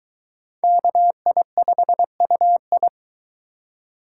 Morse code, heard as a keyed tone, about 23 wpm, 700 Hz. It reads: KI5UI